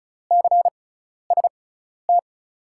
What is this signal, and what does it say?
Morse code, heard as a keyed tone, about 36 wpm, spaced as for 12 wpm, 700 Hz. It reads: CST
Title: USA Call Signs - 6 Characters - 6x Character Spacing - 35wpm